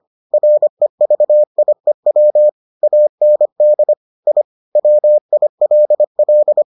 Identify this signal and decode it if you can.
Morse code, heard as a keyed tone, about 25 wpm, 600 Hz. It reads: REVIEW AND I WILL